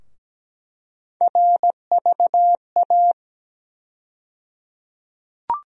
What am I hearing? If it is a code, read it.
Morse code, heard as a keyed tone, about 17 wpm, 700 Hz. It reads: RVA
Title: USA Call Sign Postfixes - 3 Letters 17wpm